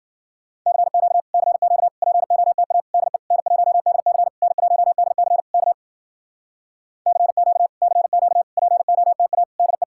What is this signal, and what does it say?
Morse code, heard as a keyed tone, about 52 wpm, 700 Hz. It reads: CQ CQ POTA DE N1DC N1DC K CQ CQ POTA DE